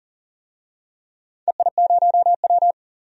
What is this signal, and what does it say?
Morse code, heard as a keyed tone, about 40 wpm, 700 Hz. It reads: EI0W
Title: DXCC-100 Call Signs - 4 Characters - 40wpm